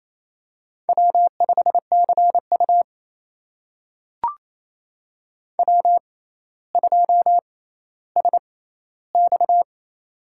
Morse code, heard as a keyed tone, about 27 wpm, spaced as for 14 wpm, 700 Hz. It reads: W5CU W2SX
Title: USA Call Signs - 4 Characters - 6x Character Spacing - 28wpm